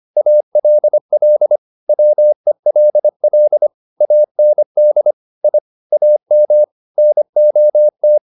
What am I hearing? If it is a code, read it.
Morse code, heard as a keyed tone, about 25 wpm, 600 Hz. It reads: ALL WELL AND I AM NOT